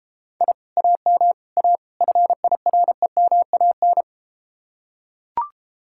Morse code, heard as a keyed tone, about 32 wpm, 700 Hz. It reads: I AM A FIREMAN